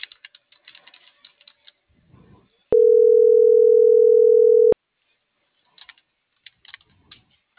Ringback tone (North America), 440 + 480 Hz.